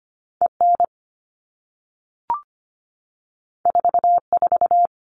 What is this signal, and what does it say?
Morse code, heard as a keyed tone, about 25 wpm, 700 Hz. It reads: EN 44
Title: All Cut Numbers - 2-Digit 25wpm